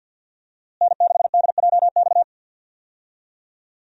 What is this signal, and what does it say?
Morse code, heard as a keyed tone, about 48 wpm, 700 Hz. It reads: N6DJX
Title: USA Call Signs - 5 Characters - Repeated 2x - 50wpm